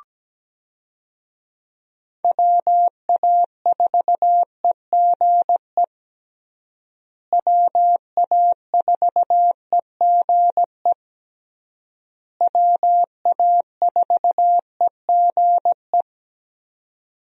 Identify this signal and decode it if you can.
Morse code, heard as a keyed tone, about 17 wpm, 700 Hz. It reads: WA4EGE WA4EGE WA4EGE
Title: USA Call Signs - 6 Characters - Repeated 3x - 17wpm